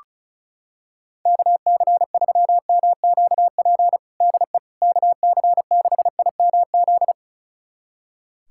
Morse code, heard as a keyed tone, about 36 wpm, 700 Hz. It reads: KC3MQP DE KC6IMZ
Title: USA Callsign DE Callsign 35wpm